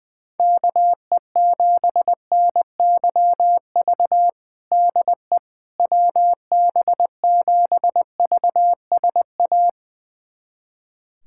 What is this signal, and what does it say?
Morse code, heard as a keyed tone, about 21 wpm, 700 Hz. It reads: KE7NYV DE WB7VSA
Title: USA Callsign DE Callsign 20wpm